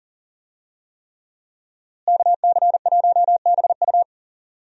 Morse code, heard as a keyed tone, about 40 wpm, 700 Hz. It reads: KC1BU
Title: USA Call Signs - 5 Characters - Repeated 3x - 40wpm